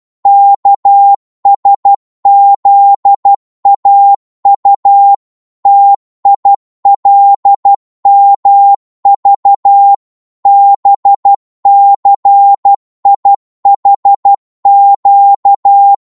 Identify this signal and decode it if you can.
Morse code, heard as a keyed tone, about 12 wpm, 800 Hz. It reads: KSZAU TILMV BCIHQ
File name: cwmp3 / ltr-12wpm